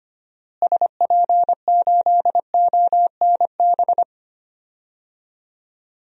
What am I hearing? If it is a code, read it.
Morse code, heard as a keyed tone, about 25 wpm, 700 Hz. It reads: SP8ONB
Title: DXCC-100 Call Signs - 6 Characters - Repeated 3x - 25wpm